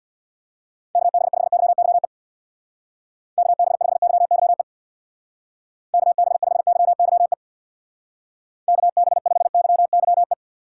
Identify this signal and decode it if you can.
Morse code, heard as a keyed tone, about 64 wpm, 700 Hz. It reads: KB5YYE KB5YYE KB5YYE KB5YYE